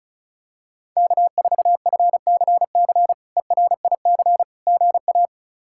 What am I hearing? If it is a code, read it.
Morse code, heard as a keyed tone, about 34 wpm, 700 Hz. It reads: K4FCC ERIC GA
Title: K1USN SST QSO Practice - 35wpm